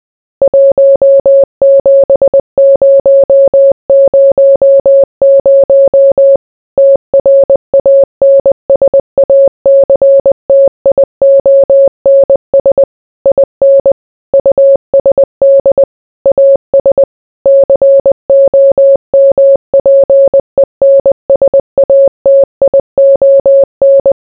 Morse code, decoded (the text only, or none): 17000 TRANSACTIONS IN USD AS COMPENSATION